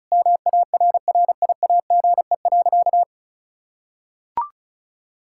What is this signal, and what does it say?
Morse code, heard as a keyed tone, about 34 wpm, 700 Hz. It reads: MARRIAGE.
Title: Sentences from Top 1000 Words - Review Part 5 - 35wpm